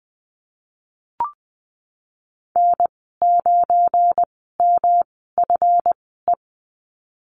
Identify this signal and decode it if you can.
Morse code, heard as a keyed tone, about 20 wpm, spaced as for 15 wpm, 700 Hz. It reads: N9MFE
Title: 238 Lesson 38g - Letter q - Call Signs 20wpm